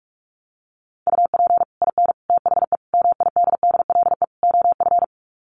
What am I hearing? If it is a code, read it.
Morse code, heard as a keyed tone, about 44 wpm, 700 Hz. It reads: UP IN THE MIDDLE OF
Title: Sets of 5 Words Encore 45wpm